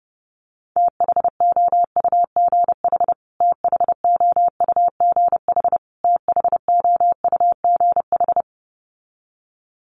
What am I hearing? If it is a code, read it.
Morse code, heard as a keyed tone, about 30 wpm, 700 Hz. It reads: THOUGH THOUGH THOUGH
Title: Top 500 Words Repeated 30wpm